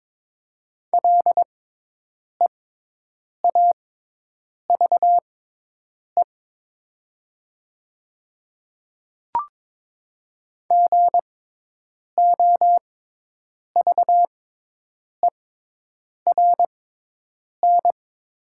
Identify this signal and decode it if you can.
Morse code, heard as a keyed tone, about 22 wpm, spaced as for 8 wpm, 700 Hz. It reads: LEAVE GOVERN
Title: 6x Character Spacing - Top 500 Words - 22wpm